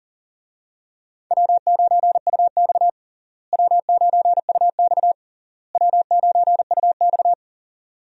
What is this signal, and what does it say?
Morse code, heard as a keyed tone, about 37 wpm, 700 Hz. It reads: W9UX W9UX W9UX
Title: USA Call Signs - 4 Characters - Repeated 3x - 40wpm